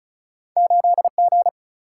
Morse code, heard as a keyed tone, about 36 wpm, 700 Hz. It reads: 8G
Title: Single-Letter Single-Number 35wpm